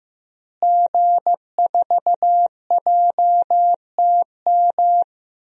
Morse code, heard as a keyed tone, about 15 wpm, 700 Hz. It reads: G4JTM
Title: DXCC-100 Call Signs - 5 Characters - Repeated 3x - 15wpm